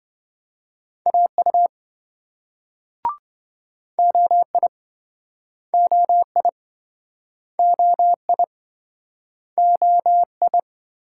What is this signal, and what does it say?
Morse code, heard as a keyed tone, about 26 wpm, 700 Hz. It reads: AU OI OI OI OI